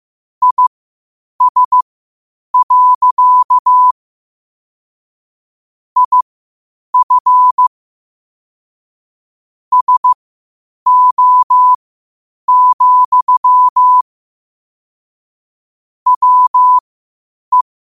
Morse code, heard as a keyed tone, about 15 wpm, spaced as for 8 wpm, 1000 Hz. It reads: IS. IF SO, WE
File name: qotdfiles / 8wpm